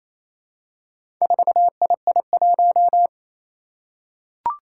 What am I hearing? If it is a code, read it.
Morse code, heard as a keyed tone, about 28 wpm, 700 Hz. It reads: 4II1
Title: Four Challenging Letter-Number Combinations 28wpm